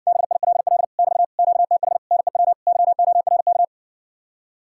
Morse code, heard as a keyed tone, about 60 wpm, 700 Hz. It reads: BILL <BT> QTH NEW YORK